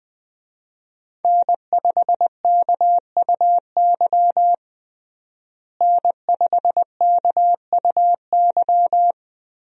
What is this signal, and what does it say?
Morse code, heard as a keyed tone, about 20 wpm, 700 Hz. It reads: N5KUY N5KUY
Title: USA Call Signs - 5 Characters - Repeated 3x - 20wpm